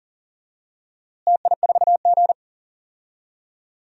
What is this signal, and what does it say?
Morse code, heard as a keyed tone, about 40 wpm, 700 Hz. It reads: TI4G
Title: DXCC-100 Call Signs - 4 Characters - Repeated 3x - 40wpm